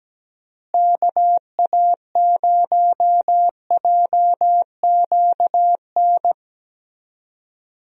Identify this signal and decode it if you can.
Morse code, heard as a keyed tone, about 17 wpm, 700 Hz. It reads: KA0JQN
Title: USA Call Signs - 6 Characters - Repeated 3x - 17wpm